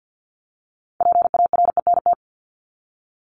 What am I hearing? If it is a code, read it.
Morse code, heard as a keyed tone, about 48 wpm, 700 Hz. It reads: PARENT
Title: Top 1751-2000 Words 50wpm